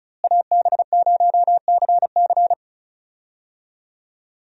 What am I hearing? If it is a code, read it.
Morse code, heard as a keyed tone, about 34 wpm, 700 Hz. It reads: AB0CC
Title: USA Call Signs - 5 Characters - Repeated 2x - 35wpm